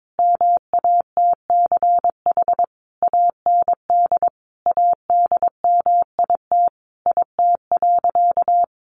Morse code, heard as a keyed tone, about 22 wpm, 700 Hz. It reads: MATCH AND ADMIT IT.
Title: A Sound Of Thunder 22wpm